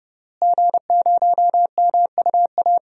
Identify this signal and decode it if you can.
Morse code, heard as a keyed tone, about 32 wpm, 700 Hz. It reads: G0MUA
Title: DXCC-100 Call Signs - 5 Characters - Repeated 3x - 30wpm